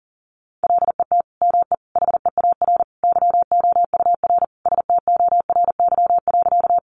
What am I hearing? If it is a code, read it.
Morse code, heard as a keyed tone, about 37 wpm, 700 Hz. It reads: LET ME HEAR YOUR STORY.
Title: Sentences from Top 300 Words 40wpm